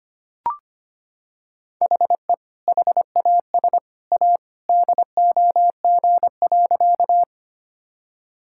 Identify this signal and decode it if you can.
Morse code, heard as a keyed tone, about 25 wpm, 700 Hz. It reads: HE HAS A DOG.
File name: practice / 25wpm / Sentences with 2-Word Phrases - 25wpm